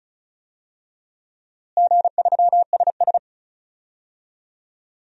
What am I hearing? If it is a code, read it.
Morse code, heard as a keyed tone, about 36 wpm, 700 Hz. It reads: G3SS
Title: DXCC-100 Call Signs - 4 Characters - Repeated 3x - 35wpm